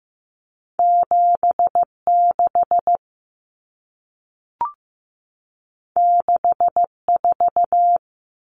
Morse code, heard as a keyed tone, about 15 wpm, 700 Hz. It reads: 76 64
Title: Numbers - 2-Digit 15wpm